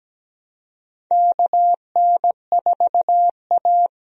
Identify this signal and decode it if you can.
Morse code, heard as a keyed tone, about 17 wpm, 700 Hz. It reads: KN4A
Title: USA Call Signs - 4 Characters - Repeated 3x - 17wpm